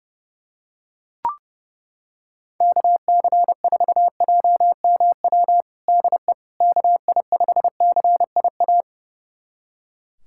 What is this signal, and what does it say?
Morse code, heard as a keyed tone, about 30 wpm, 700 Hz. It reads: KC4JMW DE KI5CIA
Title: USA Callsign DE Callsign 30wpm